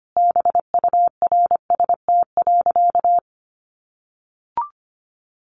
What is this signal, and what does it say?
Morse code, heard as a keyed tone, about 25 wpm, 700 Hz. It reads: BURST.
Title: A Sound Of Thunder 25wpm